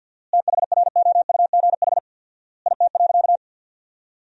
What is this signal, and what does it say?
Morse code, heard as a keyed tone, about 53 wpm, 700 Hz. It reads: THROUGH IT.
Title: 4x Word-Spacing - Sentences from Top 200 Words - 50wpm